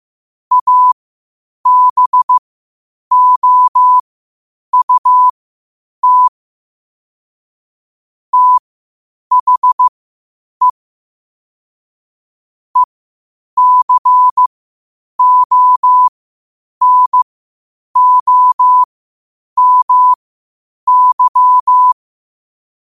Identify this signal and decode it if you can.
Morse code, heard as a keyed tone, about 15 wpm, spaced as for 8 wpm, 1000 Hz. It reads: ABOUT THE ECONOMY
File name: qotdfiles / 8wpm